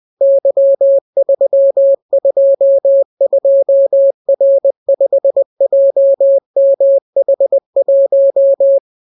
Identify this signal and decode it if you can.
Morse code, heard as a keyed tone, about 20 wpm, 550 Hz. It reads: Y322R5JMH1